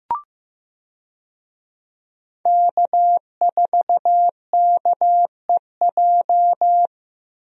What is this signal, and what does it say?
Morse code, heard as a keyed tone, about 16 wpm, 700 Hz. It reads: K4KEJ
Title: USA Call Signs - 5 Characters - 15wpm